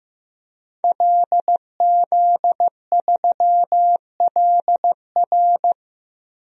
Morse code, heard as a keyed tone, about 15 wpm, 700 Hz. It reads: LZ3LR
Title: DXCC-100 Call Signs - 5 Characters - 15wpm